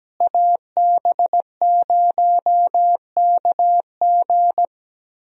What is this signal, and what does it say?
Morse code, heard as a keyed tone, about 17 wpm, 700 Hz. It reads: AB0KG
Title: USA Call Signs - 5 Characters - Repeated 3x - 17wpm